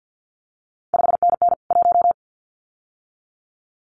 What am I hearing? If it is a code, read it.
Morse code, heard as a keyed tone, about 50 wpm, 700 Hz. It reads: 5NN 1
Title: CQ WW DX Contest Exchange - 50wpm